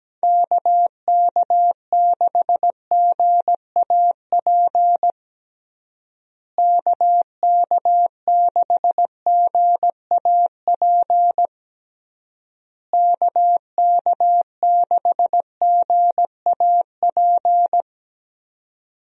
Morse code, heard as a keyed tone, about 17 wpm, 700 Hz. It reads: KK6GAP KK6GAP KK6GAP
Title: USA Call Signs - 6 Characters - Repeated 3x - 17wpm